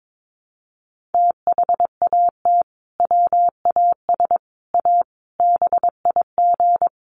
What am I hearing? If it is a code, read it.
Morse code, heard as a keyed tone, about 22 wpm, 700 Hz. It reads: THAT WAS A BIG
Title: Sentences from Top 1400 Words 22wpm